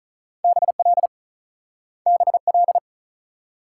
Morse code, heard as a keyed tone, about 46 wpm, 700 Hz. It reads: BL BL